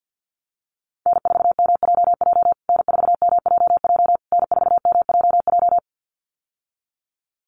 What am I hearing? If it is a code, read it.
Morse code, heard as a keyed tone, about 50 wpm, 700 Hz. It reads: N4MJJ N4MJJ N4MJJ
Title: USA Call Signs Repeated 3x 50wpm